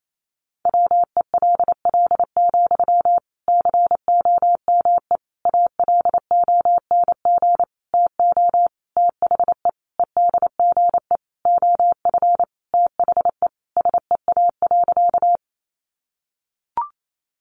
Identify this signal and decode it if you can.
Morse code, heard as a keyed tone, about 28 wpm, 700 Hz. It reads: WELL, COME ALONG TO THE EDGE OF THE SEA.
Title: Sentences from Top 1300 Words 28wpm